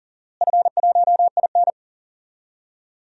Morse code, heard as a keyed tone, about 40 wpm, 700 Hz. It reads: F1IN
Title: DXCC-100 Call Signs - 4 Characters - Repeated 3x - 40wpm